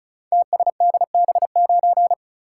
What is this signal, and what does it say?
Morse code, heard as a keyed tone, about 36 wpm, 700 Hz. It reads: TSDB9
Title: Code Groups - 5 Letter:Numbers - 35wpm